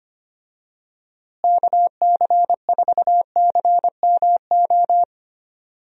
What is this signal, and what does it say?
Morse code, heard as a keyed tone, about 25 wpm, 700 Hz. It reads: KC4CMO
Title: USA Call Signs - 6 Characters - Repeated 3x - 25wpm